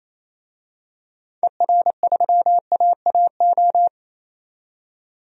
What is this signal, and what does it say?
Morse code, heard as a keyed tone, about 28 wpm, 700 Hz. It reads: ER3AAO
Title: DXCC-100 Call Signs - 6 Characters - 28wpm